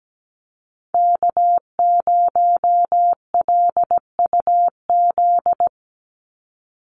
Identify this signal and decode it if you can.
Morse code, heard as a keyed tone, about 17 wpm, 700 Hz. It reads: K0LUZ
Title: Call Signs Repeated 2x - 17wpm